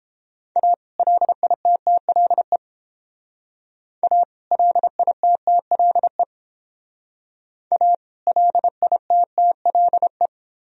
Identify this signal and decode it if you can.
Morse code, heard as a keyed tone, about 30 wpm, 700 Hz. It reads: A LITTLE A LITTLE A LITTLE